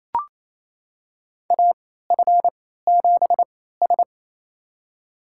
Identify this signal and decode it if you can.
Morse code, heard as a keyed tone, about 28 wpm, spaced as for 16 wpm, 700 Hz. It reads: AF7S
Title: USA Call Signs - 3x Character Spacing - 28wpm